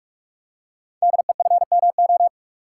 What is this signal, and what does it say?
Morse code, heard as a keyed tone, about 44 wpm, 700 Hz. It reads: DEFMO
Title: Code Groups - 5 Letters - 45wpm